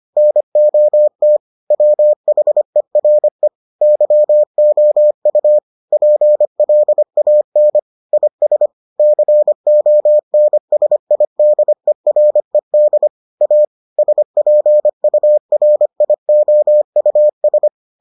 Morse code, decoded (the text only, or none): NOT WHERE YOU PLAN IS CONSIDERED A SPURIOUS